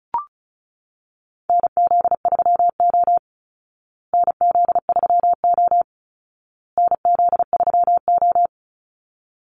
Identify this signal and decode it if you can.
Morse code, heard as a keyed tone, about 34 wpm, 700 Hz. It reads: NZ3O NZ3O NZ3O
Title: Call Signs Repeated 2x - 35wpm